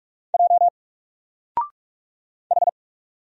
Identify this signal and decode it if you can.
Morse code, heard as a keyed tone, about 44 wpm, 700 Hz. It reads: J H
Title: Single Character (Letter:Number:Punctuation:Prosign) 45wpm